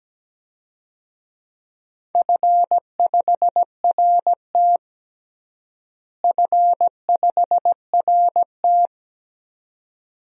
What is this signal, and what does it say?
Morse code, heard as a keyed tone, about 17 wpm, 700 Hz. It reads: F5RT F5RT
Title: DXCC-100 Call Signs - 4 Characters - Repeated 2x - 17wpm